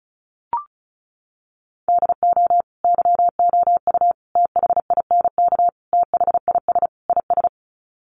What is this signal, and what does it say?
Morse code, heard as a keyed tone, about 34 wpm, 700 Hz. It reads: DO YOU THINK THIS IS
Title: Sets of 5 Words Encore2 35wpm